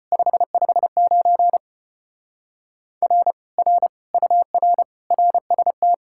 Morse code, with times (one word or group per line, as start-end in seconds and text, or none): 0.1-1.6 559
3.0-3.3 R
3.6-3.9 R
4.1-4.8 UR
5.1-5.9 RST